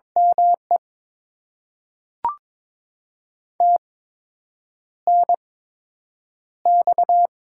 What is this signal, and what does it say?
Morse code, heard as a keyed tone, about 22 wpm, spaced as for 8 wpm, 700 Hz. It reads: ME TNX
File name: Top 20 QSO Elements - 8x Character Spacing - 22wpm